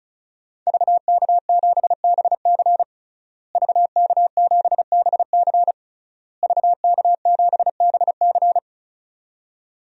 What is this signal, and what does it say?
Morse code, heard as a keyed tone, about 36 wpm, 700 Hz. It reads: VK7BC VK7BC VK7BC
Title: DXCC-100 Call Signs - 5 Characters - Repeated 3x - 35wpm